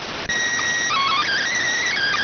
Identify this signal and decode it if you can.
SSTV (slow-scan television), PD 290.